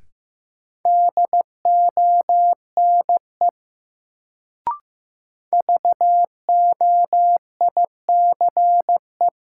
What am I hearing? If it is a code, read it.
Morse code, heard as a keyed tone, about 15 wpm, 700 Hz. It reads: DONE VOICE
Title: Sentences from Top 500 Words - New Word Drill - 15wpm